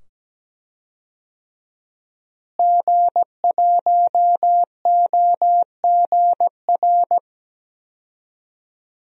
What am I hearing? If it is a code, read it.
Morse code, heard as a keyed tone, about 17 wpm, 700 Hz. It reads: G1OGR
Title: DXCC-100 Call Signs - 5 Characters - Repeated 2x - 17wpm